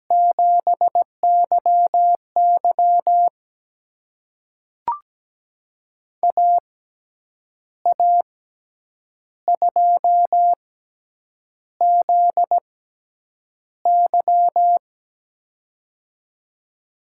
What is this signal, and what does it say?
Morse code, heard as a keyed tone, about 17 wpm, spaced as for 8 wpm, 700 Hz. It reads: 7YY AA2ZY
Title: USA Call Signs - 5 Characters - 6x Character Spacing - 17wpm